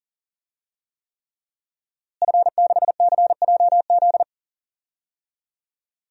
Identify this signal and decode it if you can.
Morse code, heard as a keyed tone, about 40 wpm, 700 Hz. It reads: F6CJZ
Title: DXCC-100 Call Signs - 5 Characters - Repeated 3x - 40wpm